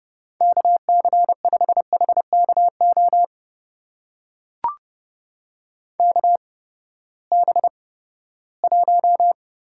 Morse code, heard as a keyed tone, about 30 wpm, spaced as for 17 wpm, 700 Hz. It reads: KC5HKO KB1
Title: USA Call Signs - 6 Characters - 8x Character Spacing - 30wpm